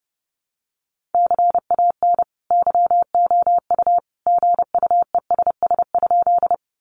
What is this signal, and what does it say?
Morse code, heard as a keyed tone, about 30 wpm, 700 Hz. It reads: CAN YOU GUESS?
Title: Sentences from Top 1500 Words 30wpm